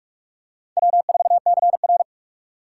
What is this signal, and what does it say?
Morse code, heard as a keyed tone, about 46 wpm, 700 Hz. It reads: W4CR
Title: USA Call Signs - 4 Characters - 2x Character Spacing - 45wpm